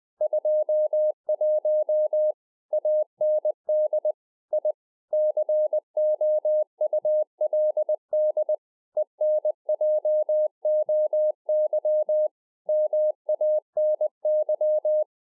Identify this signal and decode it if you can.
Morse code, heard as a keyed tone, about 20 wpm, 610 Hz. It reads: 21 AND I COULD ENJOY MANY